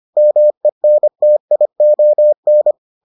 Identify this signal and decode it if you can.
Morse code, heard as a keyed tone, about 25 wpm, 600 Hz. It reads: MENTION